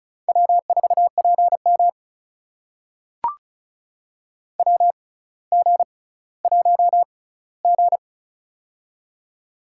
Morse code, heard as a keyed tone, about 36 wpm, spaced as for 18 wpm, 700 Hz. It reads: W4PM WG1G
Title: USA Call Signs - 4 Characters - 6x Character Spacing - 35wpm